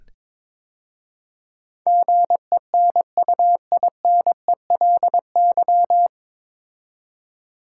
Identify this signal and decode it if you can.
Morse code, heard as a keyed tone, about 22 wpm, 700 Hz. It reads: GENUINELY
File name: 9 Letter Words - First 3 Letters Given - 22wpm